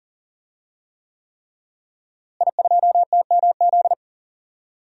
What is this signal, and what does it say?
Morse code, heard as a keyed tone, about 37 wpm, 700 Hz. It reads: I2TMZ
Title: DXCC-100 Call Signs - 5 Characters - Repeated 3x - 40wpm